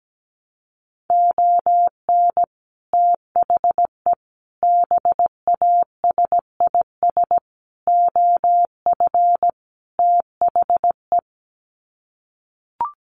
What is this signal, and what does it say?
Morse code, heard as a keyed tone, about 17 wpm, 700 Hz. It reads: ON THE BASIS OF THE
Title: Sets of 5 Words 17wpm